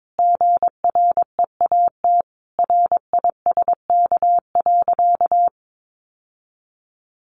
Morse code, heard as a keyed tone, about 22 wpm, 700 Hz. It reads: GREAT RISK.
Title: Sentences from Top 600 Words 22wpm